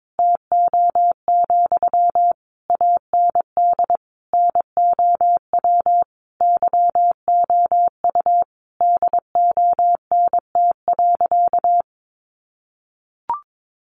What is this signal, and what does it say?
Morse code, heard as a keyed tone, about 22 wpm, 700 Hz. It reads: TO, AND NOW YOU DONT.
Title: Sentences from Top 600 Words 22wpm